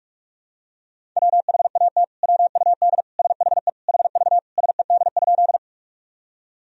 Morse code, heard as a keyed tone, about 46 wpm, 700 Hz. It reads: WHAT WUD SHE HV SED?
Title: Sentences from Top 100 Words and Abbreviations 45wpm